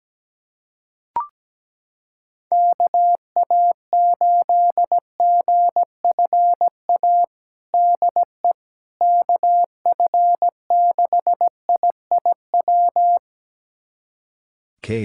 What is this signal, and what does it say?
Morse code, heard as a keyed tone, about 17 wpm, 700 Hz. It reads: KA8GFA DE KF6IIW
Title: USA Callsign DE Callsign 17wpm